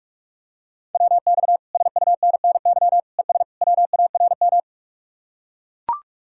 Morse code, heard as a keyed tone, about 44 wpm, 700 Hz. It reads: WX SUNNY ES WARM